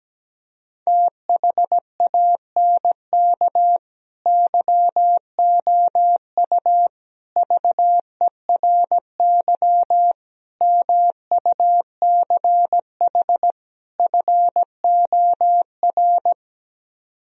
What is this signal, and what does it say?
Morse code, heard as a keyed tone, about 17 wpm, 700 Hz. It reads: THANK YOU VERY MUCH FOR
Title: Speed-Racing - Sets of 5 Words 17wpm